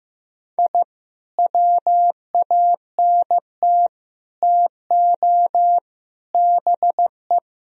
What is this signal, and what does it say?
Morse code, heard as a keyed tone, about 15 wpm, 700 Hz. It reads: I WANT TO BE
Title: Sentences from Top 1000 Words - Review Part 3 - 15wpm